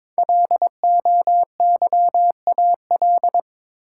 Morse code, heard as a keyed tone, about 23 wpm, 700 Hz. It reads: LOYAL
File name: 5 Letter Words - First 3 Letters Given - 22wpm